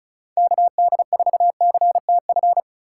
Morse code, heard as a keyed tone, about 44 wpm, 700 Hz. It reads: KD4CTF
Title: Speed Racing - USA 2x3 Call Signs 35wpm